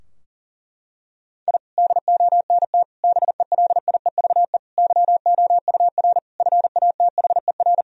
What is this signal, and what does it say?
Morse code, heard as a keyed tone, about 40 wpm, 700 Hz. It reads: I DONT BELIEVE YOUR FATHER
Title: Sentences from Top 1000 Words - Review Part 3 - 40wpm